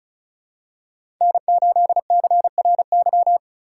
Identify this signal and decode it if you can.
Morse code, heard as a keyed tone, about 34 wpm, 700 Hz. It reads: N8CRY
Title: USA Call Signs - 6x Character Spacing - 35wpm